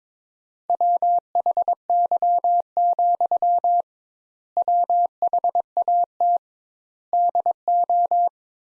Morse code, heard as a keyed tone, about 22 wpm, 700 Hz. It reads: WHY, WHAT DO
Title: Sentences Top 200 Words - 22wpm